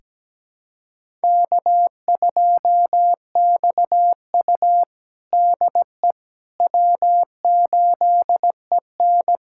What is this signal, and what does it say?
Morse code, heard as a keyed tone, about 17 wpm, 700 Hz. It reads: K2XU DE W8EN
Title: USA Callsign DE Callsign - 4 Characters - 17wpm